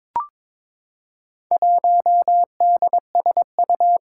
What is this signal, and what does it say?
Morse code, heard as a keyed tone, about 22 wpm, 700 Hz. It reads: 1DSU
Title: Four Challenging Letter-Number Combinations 22wpm